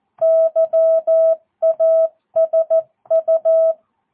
Morse code, heard as a keyed tone, about 14 wpm, 650 Hz. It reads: YASU
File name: jh8cbh / 2022cw